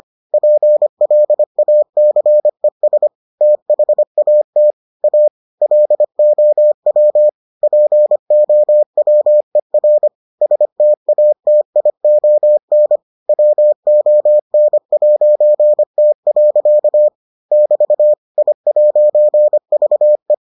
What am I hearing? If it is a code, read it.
Morse code, heard as a keyed tone, about 25 wpm, 600 Hz. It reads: PLACES THAT A LOW POWER STATION WON'T. <BT> I'VE